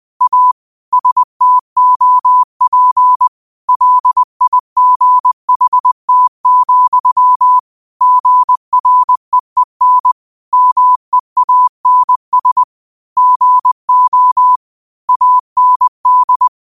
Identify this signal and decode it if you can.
Morse code, heard as a keyed tone, about 20 wpm, 1000 Hz. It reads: A STOP LIGHT, GREEN MEANS GO AND
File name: qotdfiles / 20wpm